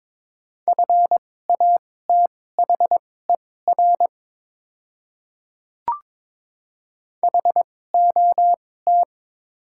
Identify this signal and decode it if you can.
Morse code, heard as a keyed tone, about 22 wpm, spaced as for 16 wpm, 700 Hz. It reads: FATHER HOT
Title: 2x Character Spacing - Top 200 Words - 22wpm